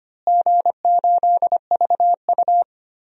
Morse code, heard as a keyed tone, about 25 wpm, 700 Hz. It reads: G8VU
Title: DXCC-100 Call Signs - 4 Characters - Repeated 3x - 25wpm